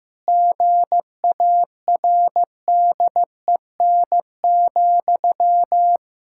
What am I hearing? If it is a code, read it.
Morse code, heard as a keyed tone, about 16 wpm, 700 Hz. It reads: GARDEN,
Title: Pangram Challenge v2 - 15wpm